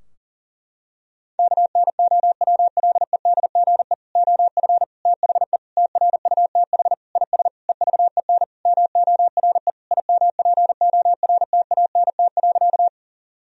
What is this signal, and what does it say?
Morse code, heard as a keyed tone, about 40 wpm, 700 Hz. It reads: KNOWLEDGE OF THE TRUTH IS EVEN MORE IMPORTANT.
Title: Sentences from Top 1000 Words - Review Part 3 - 40wpm